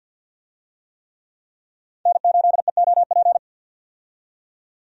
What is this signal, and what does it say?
Morse code, heard as a keyed tone, about 52 wpm, 700 Hz. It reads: N8EOP
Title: USA Call Signs - 5 Characters - 50wpm